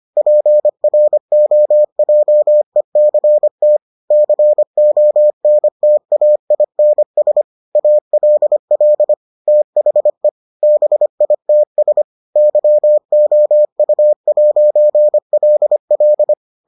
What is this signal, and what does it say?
Morse code, heard as a keyed tone, about 25 wpm, 600 Hz. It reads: PROJECT CONTAINS ALL THE BITS YOU'LL